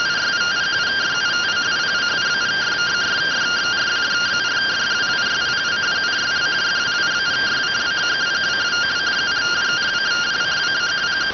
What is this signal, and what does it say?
Two-tone teleprinter signal, about 170 Hz shift.